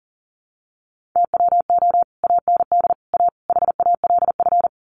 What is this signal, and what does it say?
Morse code, heard as a keyed tone, about 40 wpm, 700 Hz. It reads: TWO AND A HALF
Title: Sets of 4 Words 40wpm